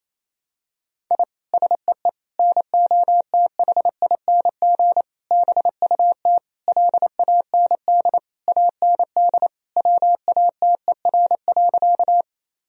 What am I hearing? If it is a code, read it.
Morse code, heard as a keyed tone, about 28 wpm, 700 Hz. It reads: I SEE NOTHING BUT LAND AND WATER.
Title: Sentences from Top 1000 Words - Review Part 1 - 28wpm